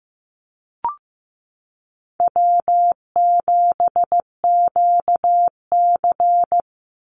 Morse code, heard as a keyed tone, about 15 wpm, 700 Hz. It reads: W7QC
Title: Easy Call Signs Repeated 2x - 15wpm